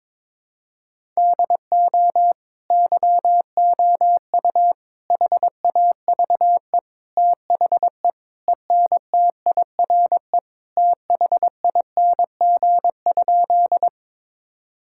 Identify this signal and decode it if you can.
Morse code, heard as a keyed tone, about 22 wpm, 700 Hz. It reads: DO YOU HAVE THE ENTIRE THING?
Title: Sentences from Top 1000 Words - Review Part 5 - 22wpm